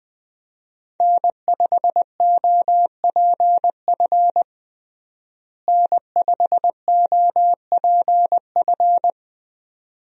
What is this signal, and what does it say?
Morse code, heard as a keyed tone, about 20 wpm, 700 Hz. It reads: N5OPF N5OPF
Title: USA Call Signs - 5 Characters - Repeated 2x - 20wpm